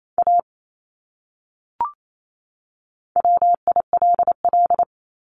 Morse code, heard as a keyed tone, about 28 wpm, 700 Hz. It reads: A WILL
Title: Top 200 Words 28wpm